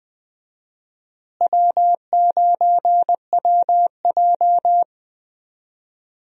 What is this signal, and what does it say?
Morse code, heard as a keyed tone, about 20 wpm, 700 Hz. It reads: W9WJ
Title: USA Call Signs - 4 Characters - Repeated 3x - 20wpm